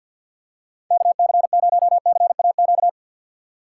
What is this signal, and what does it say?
Morse code, heard as a keyed tone, about 51 wpm, 700 Hz. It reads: KC0CAQ
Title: USA Call Signs - 6 Characters - 3x Character Spacing - 50wpm